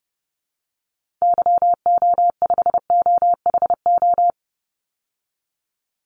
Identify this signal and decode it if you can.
Morse code, heard as a keyed tone, about 30 wpm, 700 Hz. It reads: YO5OHO
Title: Call Signs Repeated 3x - 30wpm